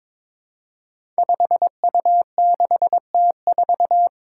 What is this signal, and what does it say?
Morse code, heard as a keyed tone, about 22 wpm, 700 Hz. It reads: 5U6T4
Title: Five Challenging Letter-Number Combinations 22wpm